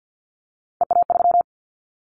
Morse code, heard as a keyed tone, about 50 wpm, 700 Hz. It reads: EA3
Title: Callsign Prefixes for Top 25 DXCC - 50wpm